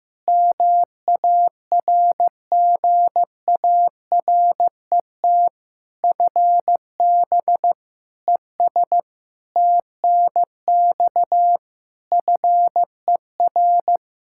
Morse code, heard as a keyed tone, about 15 wpm, 700 Hz. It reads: MARGARET FB ES TNX FER
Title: Practice QSO - Exchange 2 - Info - 15wpm